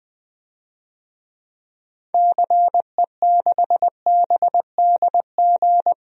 Morse code, heard as a keyed tone, about 20 wpm, 700 Hz. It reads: CE6BDG
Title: DXCC-100 Call Signs - 6 Characters - Repeated 3x - 20wpm